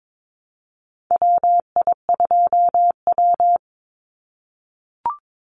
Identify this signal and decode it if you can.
Morse code, heard as a keyed tone, about 22 wpm, 700 Hz. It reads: WI2W
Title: NA SOTA Chaser Callsigns 22wpm